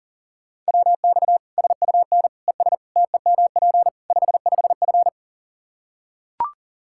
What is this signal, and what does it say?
Morse code, heard as a keyed tone, about 40 wpm, 700 Hz. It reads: WX SUN ES TEMP 55F
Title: K1USN SST QSO Practice 40wpm